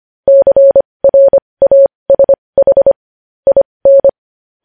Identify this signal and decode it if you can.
Morse code, heard as a keyed tone, about 25 wpm, spaced as for 20 wpm, 550 Hz. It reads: CRASH IN
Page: News Headlines in Morse Code at 20 WPM